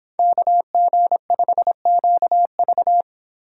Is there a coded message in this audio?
Yes: Morse code from 0.2 to 3.0 s, heard as a keyed tone, about 25 wpm, 700 Hz. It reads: KG5QV